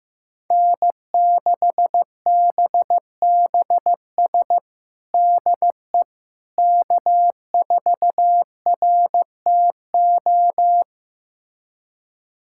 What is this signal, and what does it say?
Morse code, heard as a keyed tone, about 15 wpm, 700 Hz. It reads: N6BBS DE K4RTO
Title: USA Callsign DE Callsign - 5 Characters - 15wpm